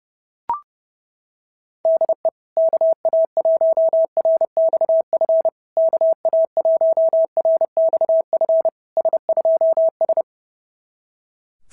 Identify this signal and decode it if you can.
Morse code, heard as a keyed tone, about 30 wpm, 690 Hz. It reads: DE KA1RXF KA1RXF S2S
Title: SOTA QSO Practice 30wpm